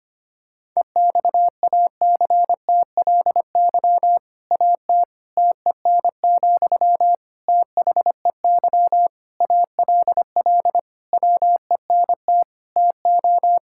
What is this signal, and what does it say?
Morse code, heard as a keyed tone, about 25 wpm, 700 Hz. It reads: EXACTLY AT TEN, THEY ALL WENT TO
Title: Sentences from Top 1000 Words - Review Part 6 - 25wpm